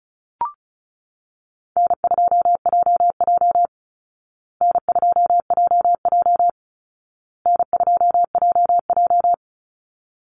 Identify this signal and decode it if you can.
Morse code, heard as a keyed tone, about 34 wpm, 700 Hz. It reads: N2JJ N2JJ N2JJ
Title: Easy Call Signs Repeated 3x - 35wpm